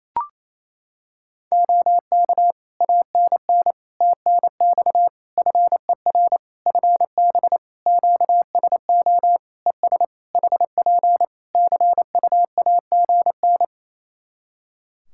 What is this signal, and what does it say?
Morse code, heard as a keyed tone, about 28 wpm, 700 Hz. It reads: OK ANN TNX FER FB QSO ES HP CUAGN
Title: Practice QSO - Exchange 4 - Ending - 28wpm